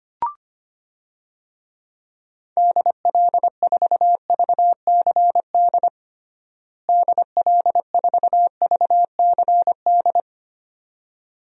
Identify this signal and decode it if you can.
Morse code, heard as a keyed tone, about 25 wpm, 700 Hz. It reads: DL4VCD DL4VCD
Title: DXCC-100 Call Signs - 6 Characters - Repeated 2x - 25wpm